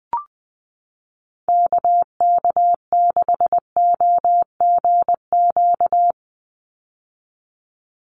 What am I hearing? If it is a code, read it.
Morse code, heard as a keyed tone, about 20 wpm, 700 Hz. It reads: KK6OGQ
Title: USA Call Signs 20wpm